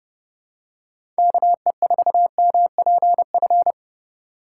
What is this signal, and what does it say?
Morse code, heard as a keyed tone, about 30 wpm, 700 Hz. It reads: KE4MPF
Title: USA Call Signs - 6x Character Spacing - 30wpm